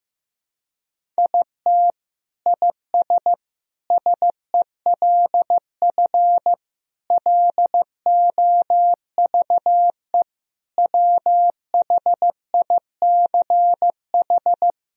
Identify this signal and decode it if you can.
Morse code, heard as a keyed tone, about 15 wpm, 700 Hz. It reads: IT IS SELF LOVE WHICH
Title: Sentences from Top 1000 Words - Review Part 1 - 15wpm